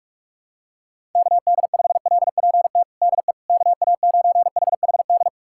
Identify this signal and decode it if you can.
Morse code, heard as a keyed tone, about 46 wpm, 700 Hz. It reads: KD5LPT DE KA9HHD
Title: USA Callsign DE Callsign 45wpm